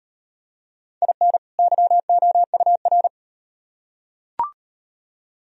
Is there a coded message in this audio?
Yes: Morse code from 1.0 to 3.1 s, heard as a keyed tone, about 36 wpm, 700 Hz. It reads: IN YOUR